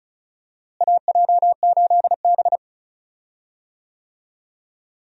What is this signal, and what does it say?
Morse code, heard as a keyed tone, about 36 wpm, 700 Hz. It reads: AJ8B
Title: USA Call Signs - 4 Characters - Repeated 2x - 35wpm